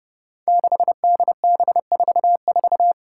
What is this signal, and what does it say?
Morse code, heard as a keyed tone, about 29 wpm, 700 Hz. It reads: 6DB44
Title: Five Challenging Letter-Number Combinations 30wpm